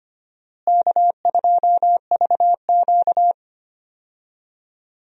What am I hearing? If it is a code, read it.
Morse code, heard as a keyed tone, about 25 wpm, 700 Hz. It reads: K2VQ
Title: USA Call Signs - 4 Characters - Repeated 2x - 25wpm